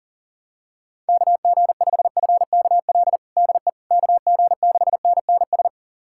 Morse code, heard as a keyed tone, about 37 wpm, 700 Hz. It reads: KG5FKL DE KG6NNS
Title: USA Callsign DE Callsign 40wpm